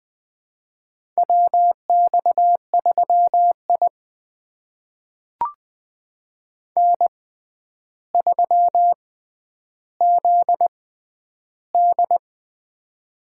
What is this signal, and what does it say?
Morse code, heard as a keyed tone, about 19 wpm, spaced as for 10 wpm, 700 Hz. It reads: WX3I N3ZD
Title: USA Call Signs - 4 Characters - 6x Character Spacing - 20wpm